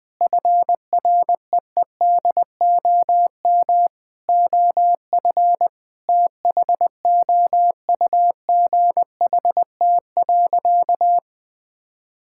Morse code, heard as a keyed tone, about 21 wpm, 700 Hz. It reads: FREEDOM OF THOUGHT.
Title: Sentences from Top 1000 Words - Review Part 5 - 20wpm